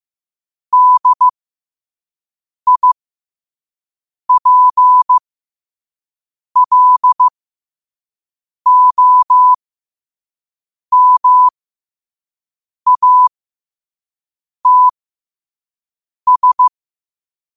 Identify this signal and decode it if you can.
Morse code, heard as a keyed tone, about 15 wpm, spaced as for 5 wpm, 1000 Hz. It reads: DIPLOMATS
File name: qotdfiles / 5wpm